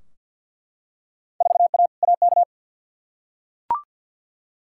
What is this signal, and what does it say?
Morse code, heard as a keyed tone, about 51 wpm, 700 Hz. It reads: 4A AK
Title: Field Day Exchanges Encore - 50wpm